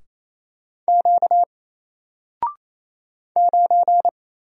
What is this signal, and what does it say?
Morse code, heard as a keyed tone, about 28 wpm, 700 Hz. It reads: Q 9
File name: Single-Letter-Number-28wpm